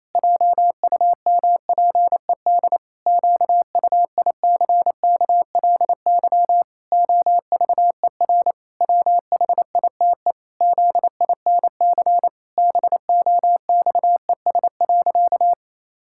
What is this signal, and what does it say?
Morse code, heard as a keyed tone, about 28 wpm, 700 Hz. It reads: JUMPED QUICKLY OVER WHITE ZINC BOXES.
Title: Pangram Challenge v2 - 28wpm